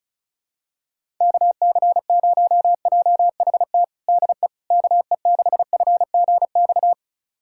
Morse code, heard as a keyed tone, about 36 wpm, 700 Hz. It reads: KC0JHT DE KE6FGX
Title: USA Callsign DE Callsign - 6 Characters - 35wpm